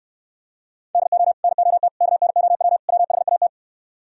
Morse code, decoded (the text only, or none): DO NOT KNOW WHAT